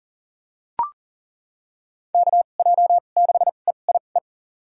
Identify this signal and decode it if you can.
Morse code, heard as a keyed tone, about 40 wpm, spaced as for 29 wpm, 700 Hz. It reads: KJ6EIE
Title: USA Call Signs - 6 Characters - 2x Character Spacing - 40wpm